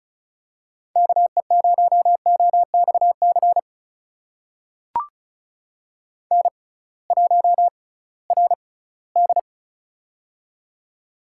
Morse code, heard as a keyed tone, about 36 wpm, spaced as for 20 wpm, 700 Hz. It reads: KE0OXC N1RD
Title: USA Call Signs - 6x Character Spacing - 35wpm